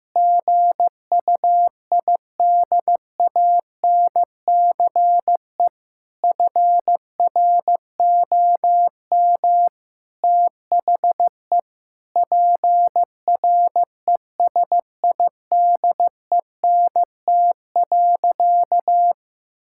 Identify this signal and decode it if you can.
Morse code, heard as a keyed tone, about 15 wpm, 700 Hz. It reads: GUIDANCE FROM THE PRESIDENT.
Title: Sentences from Top 1600 Words 15wpm